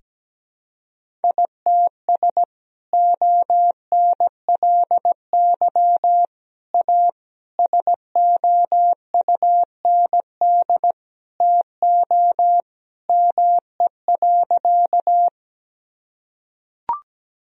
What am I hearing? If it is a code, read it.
Morse code, heard as a keyed tone, about 17 wpm, 700 Hz. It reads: ITS ONLY A SOUND TO ME.
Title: Sentences from Top 1000 Words - Review Part 1 - 17wpm